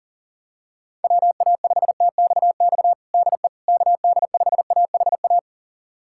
Morse code, heard as a keyed tone, about 40 wpm, 700 Hz. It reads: WA5TXX DE KD5AHA
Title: USA Callsign DE Callsign - 6 Characters - 40wpm